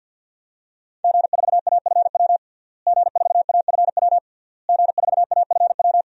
Morse code, heard as a keyed tone, about 51 wpm, 700 Hz. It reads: G4AFW G4AFW G4AFW
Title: DXCC-100 Call Signs - 5 Characters - Repeated 3x - 50wpm